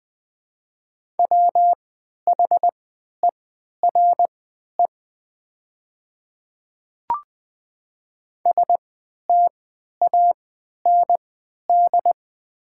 Morse code, heard as a keyed tone, about 20 wpm, spaced as for 11 wpm, 700 Hz. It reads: WHERE STAND
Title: 3x Character Spacing - Top 300 Words - 20wpm